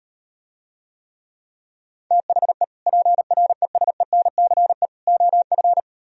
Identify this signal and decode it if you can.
Morse code, heard as a keyed tone, about 38 wpm, 700 Hz. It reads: THE PRESENCE OF